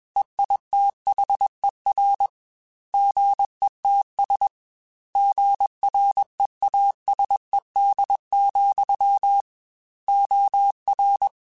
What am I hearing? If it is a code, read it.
Morse code, heard as a keyed tone, about 21 wpm, 800 Hz. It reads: EITHER GETS GREASED, OR